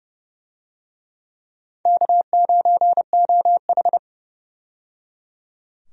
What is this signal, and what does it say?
Morse code, heard as a keyed tone, about 30 wpm, 700 Hz. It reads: K9OH
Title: USA Call Signs - 4 Characters - 30wpm